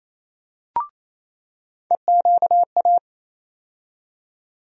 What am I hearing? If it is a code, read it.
Morse code, heard as a keyed tone, about 27 wpm, 700 Hz. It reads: EQA
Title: USA Call Sign Postfixes - 3 Letters 28wpm